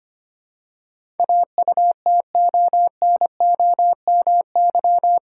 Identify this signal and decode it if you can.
Morse code, heard as a keyed tone, about 25 wpm, 700 Hz. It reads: AUTONOMY
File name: Eight Letter Words - 25wpm